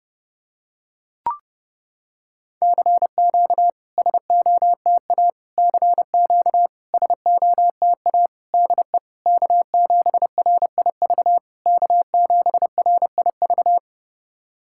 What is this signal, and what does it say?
Morse code, heard as a keyed tone, about 30 wpm, 700 Hz. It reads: CQ SOTA CQ SOTA DE K7RIV K7RIV
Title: SOTA QSO Practice 30wpm